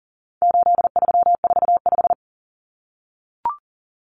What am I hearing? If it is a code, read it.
Morse code, heard as a keyed tone, about 40 wpm, 700 Hz. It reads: 8345
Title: Common Cut Numbers - 4-Digit 40wpm